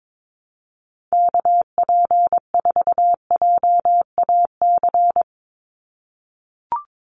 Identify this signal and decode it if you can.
Morse code, heard as a keyed tone, about 22 wpm, 700 Hz. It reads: KP4JAC
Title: USA Call Signs 22wpm